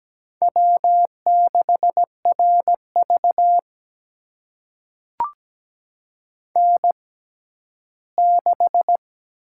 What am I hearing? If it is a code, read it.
Morse code, heard as a keyed tone, about 17 wpm, spaced as for 12 wpm, 700 Hz. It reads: W6RV N6
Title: USA Call Signs - 4 Characters - 6x Character Spacing - 17wpm